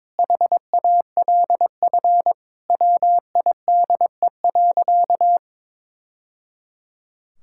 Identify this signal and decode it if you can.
Morse code, heard as a keyed tone, about 23 wpm, 700 Hz. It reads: HALF WIDE.
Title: Sentences from Top 1000 Words - Review Part 6 - 22wpm